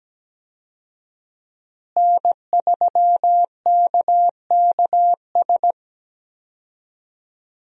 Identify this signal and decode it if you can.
Morse code, heard as a keyed tone, about 17 wpm, 700 Hz. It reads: N3KKS
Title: USA Call Signs - 5 Characters - Repeated 3x - 17wpm